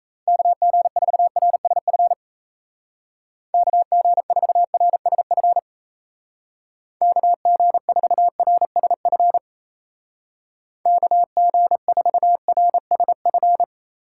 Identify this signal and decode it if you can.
Morse code, heard as a keyed tone, about 35 wpm, 700 Hz. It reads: KG4RSF KG4RSF KG4RSF KG4RSF